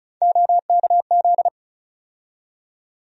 Morse code, heard as a keyed tone, about 34 wpm, 700 Hz. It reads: OKZ
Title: USA Call Sign Postfixes - 3 Letters 35wpm